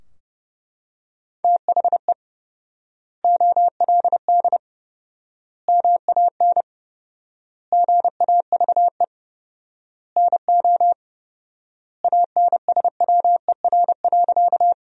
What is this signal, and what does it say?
Morse code, heard as a keyed tone, about 30 wpm, 700 Hz. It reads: THE OLD MAN GAVE NO ANSWER.
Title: 4x Word-Spacing - Sentences from Top 500 Words - 30wpm